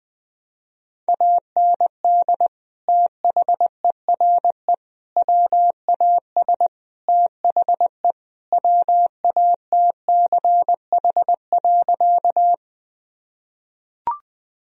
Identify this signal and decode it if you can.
Morse code, heard as a keyed tone, about 21 wpm, 700 Hz. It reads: AND THERE WAS THE WATCH.
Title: Sentences from Top 1000 Words - Review Part 1 - 20wpm